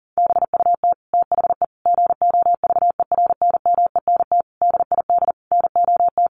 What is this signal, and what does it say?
Morse code, heard as a keyed tone, about 40 wpm, 700 Hz. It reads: BUT THE GOVERNMENT DID NOT
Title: Sentences from Top 1400 Words 40wpm